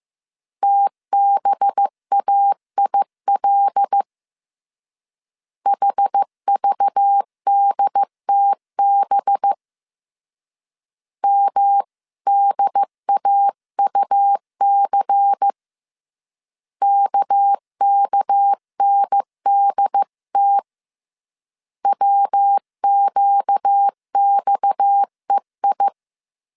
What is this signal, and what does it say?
Morse code, heard as a keyed tone, about 15 wpm, 800 Hz. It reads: TBAIL HVDTB MDAUC KKNDT WQXEI